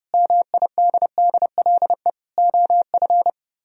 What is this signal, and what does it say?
Morse code, heard as a keyed tone, about 28 wpm, 700 Hz. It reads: MIDDLE OF